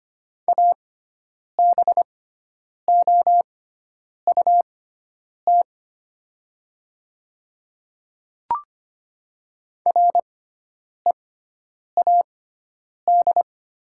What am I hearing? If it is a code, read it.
Morse code, heard as a keyed tone, about 25 wpm, spaced as for 9 wpm, 700 Hz. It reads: ABOUT READ
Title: 6x Character Spacing - Top 200 Words - 25wpm